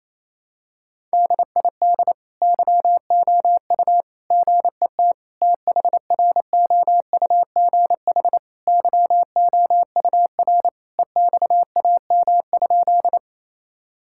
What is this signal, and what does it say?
Morse code, heard as a keyed tone, about 28 wpm, 700 Hz. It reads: DID YOU GET THROUGH YOUR EXAM?
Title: Sentences from Top 1000 Words - Review Part 5 - 28wpm